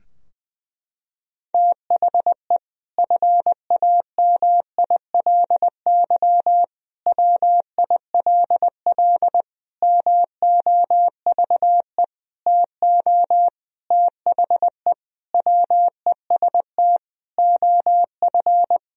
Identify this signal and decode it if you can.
Morse code, heard as a keyed tone, about 20 wpm, 700 Hz. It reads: THE FAMILY WILL MOVE TO THE WEST OF
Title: Sentences from Top 1000 Words - Review Part 2 - 20wpm